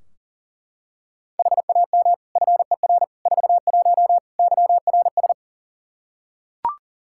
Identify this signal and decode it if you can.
Morse code, heard as a keyed tone, about 40 wpm, 690 Hz. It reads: HAM FER 41 YRS
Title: Practice QSO - Exchange 3 - More Info - 40wpm